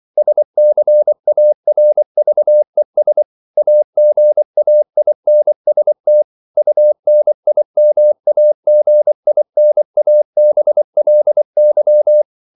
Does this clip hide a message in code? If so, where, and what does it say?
Yes: Morse code from 0.2 to 12.2 s, heard as a keyed tone, about 24 wpm, 600 Hz. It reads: SCARVES AGAINST UNIMAGINABLY